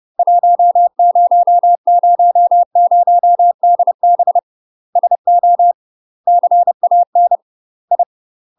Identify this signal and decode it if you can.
Morse code, heard as a keyed tone, about 30 wpm, 700 Hz. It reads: 1000DB SO CAN I
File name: UK MASTER QSOs TRAINING / 30WPM / UK MASTER QSOs 0019